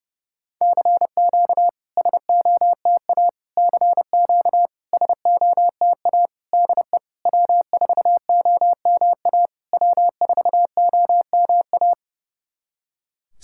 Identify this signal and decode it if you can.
Morse code, heard as a keyed tone, about 29 wpm, 700 Hz. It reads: CQ SOTA CQ SOTA DE W4OMA W4OMA
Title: SOTA QSO Practice 30wpm